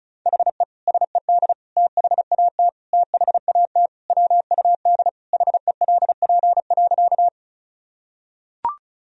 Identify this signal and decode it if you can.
Morse code, heard as a keyed tone, about 34 wpm, 700 Hz. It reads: HE SED THAT THAT WUD HELP.
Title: Sentences from Top 200 Words and Abbreviations 35wpm